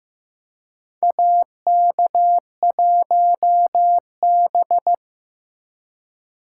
Morse code, heard as a keyed tone, about 16 wpm, 700 Hz. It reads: AK1B
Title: USA Call Signs - 2x Character Spacing - 15wpm